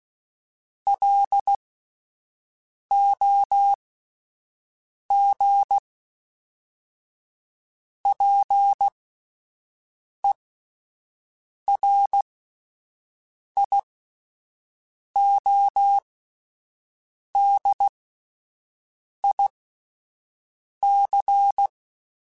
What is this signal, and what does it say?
Morse code, heard as a keyed tone, about 16 wpm, spaced as for 5 wpm, 800 Hz. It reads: LOG PERIODIC
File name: fromSMs / FromSMs mp3s / Antennas@5